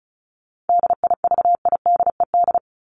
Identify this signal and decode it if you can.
Morse code, heard as a keyed tone, about 34 wpm, 700 Hz. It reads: DIVIDED
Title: ED Suffix - Top 100 Words with Base Words 35wpm